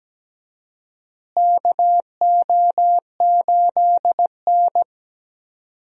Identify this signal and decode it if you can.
Morse code, heard as a keyed tone, about 17 wpm, 700 Hz. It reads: KO8N
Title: USA Call Signs - 4 Characters - Repeated 3x - 17wpm